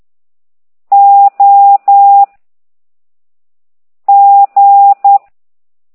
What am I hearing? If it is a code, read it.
Morse code, heard as a keyed tone, about 10 wpm, 800 Hz. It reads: O G